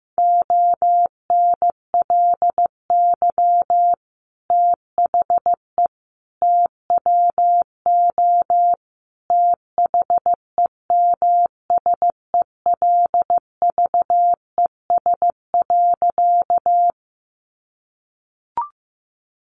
Morse code, heard as a keyed tone, about 15 wpm, 700 Hz. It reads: ONLY THE TWO THEMSELVES.
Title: Sentences from Top 800 Words 15wpm